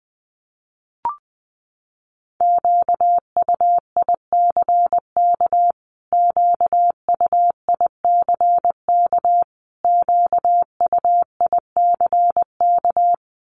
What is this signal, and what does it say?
Morse code, heard as a keyed tone, about 20 wpm, 700 Hz. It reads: QUICK QUICK QUICK
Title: Top 500 Words Repeated 20wpm